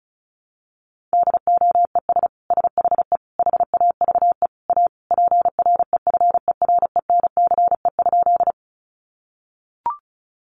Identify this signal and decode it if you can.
Morse code, heard as a keyed tone, about 34 wpm, 700 Hz. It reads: DOES SHE HAVE A PREFERENCE?
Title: Sentences from Top 1500 Words 35wpm